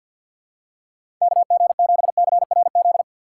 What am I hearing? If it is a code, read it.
Morse code, heard as a keyed tone, about 52 wpm, 700 Hz. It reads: KG7CRZ
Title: USA Call Signs - 6 Characters - 4x Character Spacing - 50wpm